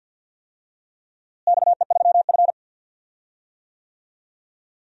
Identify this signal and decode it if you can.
Morse code, heard as a keyed tone, about 50 wpm, 700 Hz. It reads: XE3F
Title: DXCC-100 Call Signs - 4 Characters - Repeated 2x - 50wpm